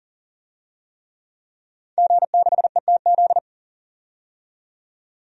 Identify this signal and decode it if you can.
Morse code, heard as a keyed tone, about 40 wpm, 700 Hz. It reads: G6ETZ
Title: DXCC-100 Call Signs - 5 Characters - 40wpm